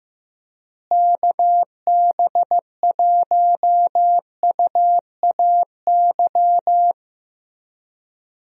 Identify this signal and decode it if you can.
Morse code, heard as a keyed tone, about 15 wpm, 700 Hz. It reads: KB1UAY
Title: USA Call Signs - 6 Characters - Repeated 2x - 15wpm